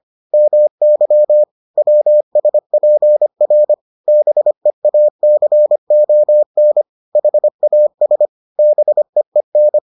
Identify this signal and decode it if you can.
Morse code, heard as a keyed tone, about 25 wpm, 600 Hz. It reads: MY WSPR BEACON HAS BEEN